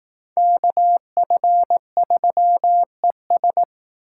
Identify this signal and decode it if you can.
Morse code, heard as a keyed tone, about 19 wpm, 700 Hz. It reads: KF3ES